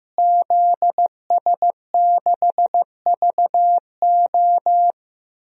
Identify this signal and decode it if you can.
Morse code, heard as a keyed tone, about 15 wpm, 700 Hz. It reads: ZS6VO
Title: DXCC-100 Call Signs - 5 Characters - Repeated 2x - 15wpm